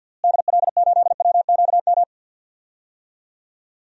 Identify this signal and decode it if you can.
Morse code, heard as a keyed tone, about 46 wpm, 700 Hz. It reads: DL8WQM
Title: DXCC-100 Call Signs - 6 Characters - Repeated 2x - 50wpm